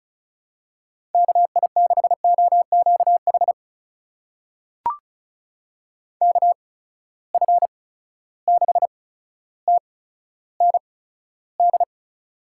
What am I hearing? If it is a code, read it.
Morse code, heard as a keyed tone, about 36 wpm, spaced as for 15 wpm, 700 Hz. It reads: KI6OQH KF6TND
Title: USA Call Signs - 6 Characters - 8x Character Spacing - 35wpm